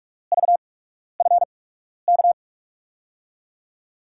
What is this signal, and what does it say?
Morse code, heard as a keyed tone, about 46 wpm, spaced as for 12 wpm, 700 Hz. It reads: VFK
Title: USA Call Signs - 6 Characters - 8x Character Spacing - 45wpm